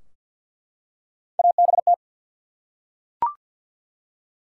Morse code, heard as a keyed tone, about 48 wpm, 700 Hz. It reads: ABT
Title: Top 90 QSO Elements 50wpm